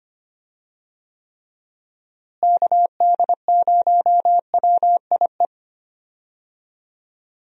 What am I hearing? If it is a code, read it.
Morse code, heard as a keyed tone, about 25 wpm, 700 Hz. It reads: KD0WIE
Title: USA Call Signs - 6 Characters - Repeated 2x - 25wpm